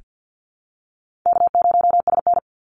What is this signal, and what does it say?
Morse code, heard as a keyed tone, about 48 wpm, 700 Hz. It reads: K0SN
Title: Easy Call Signs Repeated 2x - 50wpm